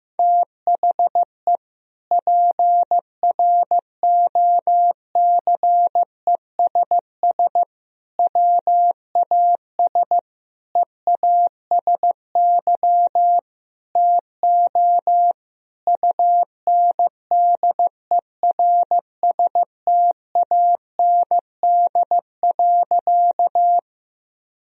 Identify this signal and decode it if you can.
Morse code, heard as a keyed tone, about 15 wpm, 700 Hz. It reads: THE PROCESS WAS EASY TO UNDERSTAND.
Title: Sentences from Top 1000 Words - Review Part 3 - 15wpm